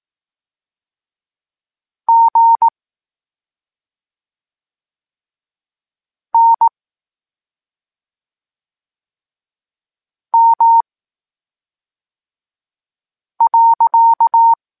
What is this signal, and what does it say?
Morse code, heard as a keyed tone, about 18 wpm, 920 Hz. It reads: G N M .
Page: Tekst 23.3 – Ciąg dalszy doskonalenia odbioru poznanych liter